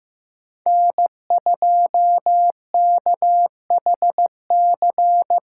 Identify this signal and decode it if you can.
Morse code, heard as a keyed tone, about 15 wpm, 700 Hz. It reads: N2KHC
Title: USA Call Signs - 8x Character Spacing - 15wpm